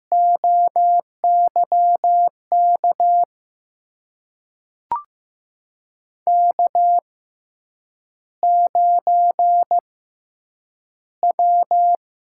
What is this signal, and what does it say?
Morse code, heard as a keyed tone, about 15 wpm, spaced as for 8 wpm, 700 Hz. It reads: OYK K9W
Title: USA Call Signs - 6x Character Spacing - 15wpm